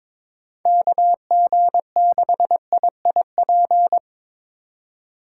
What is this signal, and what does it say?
Morse code, heard as a keyed tone, about 22 wpm, 700 Hz. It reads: KG6IIP
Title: K1USN SST QSO Practice - 22wpm